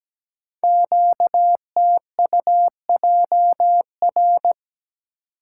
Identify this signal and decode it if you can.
Morse code, heard as a keyed tone, about 17 wpm, 700 Hz. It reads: QTUJR
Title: Code Groups - 5 Letters - 17wpm